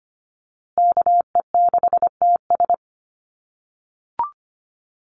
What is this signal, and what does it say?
Morse code, heard as a keyed tone, about 25 wpm, 700 Hz. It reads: KE6TS
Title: USA Call Signs Repeated 3x 25wpm